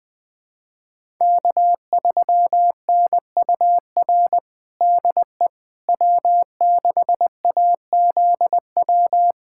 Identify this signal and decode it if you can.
Morse code, heard as a keyed tone, about 20 wpm, 700 Hz. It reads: K3NUR DE W6AZW
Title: USA Callsign DE Callsign - 5 Characters - 20wpm